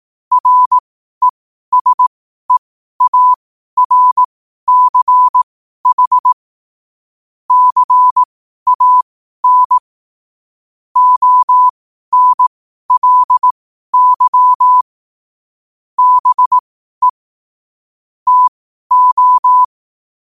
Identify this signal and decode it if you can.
Morse code, heard as a keyed tone, about 18 wpm, spaced as for 12 wpm, 1000 Hz. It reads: RESEARCH CAN ONLY BE TO
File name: qotdfiles / 12wpm